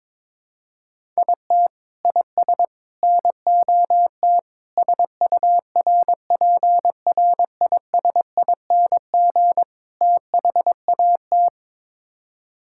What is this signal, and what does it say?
Morse code, heard as a keyed tone, about 22 wpm, 700 Hz. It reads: IT IS NOT SURPRISING THAT
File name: practice / 22wpm / Build A Phrase - 5-Word Phrases - 22wpm